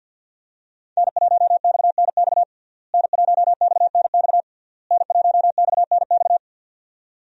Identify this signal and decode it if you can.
Morse code, heard as a keyed tone, about 53 wpm, 700 Hz. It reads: N1XNX N1XNX N1XNX
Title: USA Call Signs - 5 Characters - Repeated 3x - 50wpm